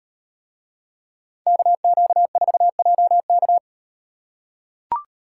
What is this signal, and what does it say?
Morse code, heard as a keyed tone, about 36 wpm, 700 Hz. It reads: KQ4JK